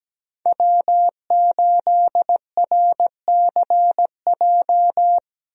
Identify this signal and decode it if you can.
Morse code, heard as a keyed tone, about 17 wpm, 700 Hz. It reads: W8RCJ
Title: USA Call Signs - 5 Characters - Repeated 3x - 17wpm